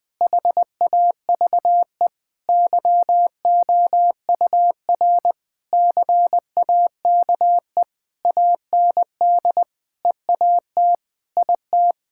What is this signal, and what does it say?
Morse code, heard as a keyed tone, about 21 wpm, 700 Hz. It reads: HAVE YOUR CAKE AND EAT IT
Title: Sentences with 9-Word Phrases - 20wpm